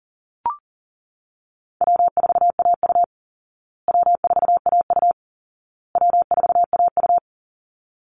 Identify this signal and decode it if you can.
Morse code, heard as a keyed tone, about 40 wpm, 700 Hz. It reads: W4AU W4AU W4AU
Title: Easy Call Signs Repeated 3x - 40wpm